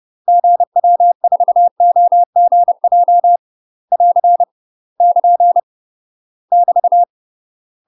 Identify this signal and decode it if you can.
Morse code, heard as a keyed tone, about 30 wpm, 700 Hz. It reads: GW4OGJ <AR> <KN> <BT>
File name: UK MASTER QSOs TRAINING / 30WPM / UK MASTER QSOs 0021